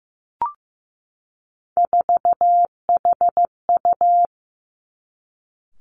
Morse code, heard as a keyed tone, about 15 wpm, 700 Hz. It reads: 4HU
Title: Three Challenging Letter-Number Combinations 15wpm